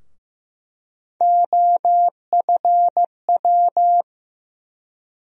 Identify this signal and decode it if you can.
Morse code, heard as a keyed tone, about 15 wpm, 700 Hz. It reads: OFW
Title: USA Call Sign Postfixes - 3 Letters 15wpm